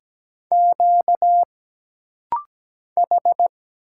Morse code, heard as a keyed tone, about 17 wpm, 700 Hz. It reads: Q H
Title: Single Letter-Number Alt - Rapid-Fire 17wpm